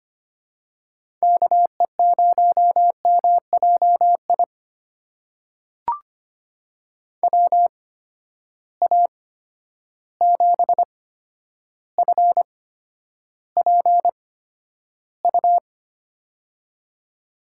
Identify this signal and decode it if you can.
Morse code, heard as a keyed tone, about 25 wpm, spaced as for 11 wpm, 700 Hz. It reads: KE0MJI WA7FPU
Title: USA Call Signs - 6 Characters - 8x Character Spacing - 25wpm